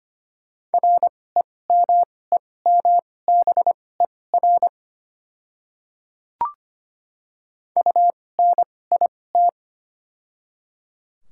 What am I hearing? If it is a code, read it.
Morse code, heard as a keyed tone, about 25 wpm, spaced as for 18 wpm, 700 Hz. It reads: REMEMBER UNIT
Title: 2x Character Spacing - Top 500 Words - 25wpm